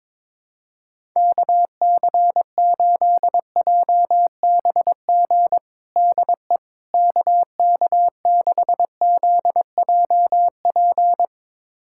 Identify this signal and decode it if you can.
Morse code, heard as a keyed tone, about 22 wpm, 700 Hz. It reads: KC8JBG DE KK6ZJP
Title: USA Callsign DE Callsign - 6 Characters - 22wpm